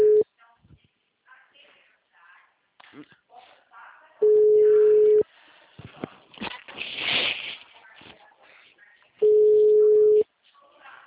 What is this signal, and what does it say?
Ringback tone (Europe, 425 Hz), with speech.
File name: out- -2017- - - .14